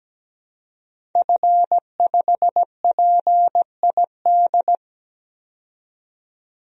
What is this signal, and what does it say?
Morse code, heard as a keyed tone, about 17 wpm, 700 Hz. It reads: F5PID
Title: DXCC-100 Call Signs - 5 Characters - 17wpm